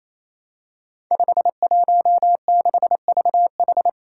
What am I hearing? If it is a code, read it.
Morse code, heard as a keyed tone, about 27 wpm, 700 Hz. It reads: 516VH
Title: Five Challenging Letter-Number Combinations 28wpm